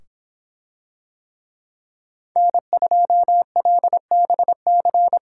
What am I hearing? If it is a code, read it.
Morse code, heard as a keyed tone, about 26 wpm, 700 Hz. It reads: N2LBC